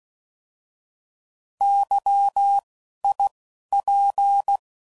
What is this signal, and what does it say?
Morse code, heard as a keyed tone, about 16 wpm, spaced as for 11 wpm, 800 Hz. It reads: YIP
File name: Prefixes / By First Letter / Ys